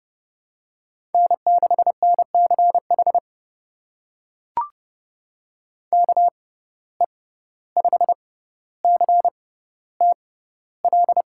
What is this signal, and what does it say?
Morse code, heard as a keyed tone, about 30 wpm, spaced as for 15 wpm, 700 Hz. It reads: N6NCH KE5CTL
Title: USA Call Signs - 6x Character Spacing - 30wpm